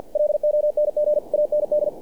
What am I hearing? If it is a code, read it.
Morse code, heard as a keyed tone, about 48 wpm, 600 Hz. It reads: LONG AND
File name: archived / CRSnet